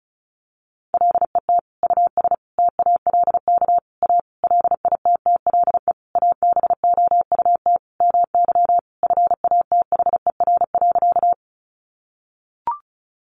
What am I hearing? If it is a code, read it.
Morse code, heard as a keyed tone, about 34 wpm, 700 Hz. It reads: LET US TALK A LITTLE ABOUT MY FATHER.
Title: Sentences from Top 400 Words 35wpm